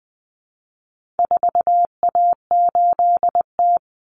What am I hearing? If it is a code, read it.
Morse code, heard as a keyed tone, about 20 wpm, 700 Hz. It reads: 4A8T
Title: Common Cut Numbers - 4-Digit 20wpm